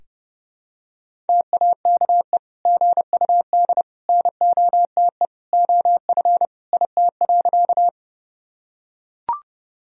Morse code, heard as a keyed tone, about 31 wpm, 700 Hz. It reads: TAKE GUD NOTE OF IT.
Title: Sentences from Top 500 Words and Abbreviations 30wpm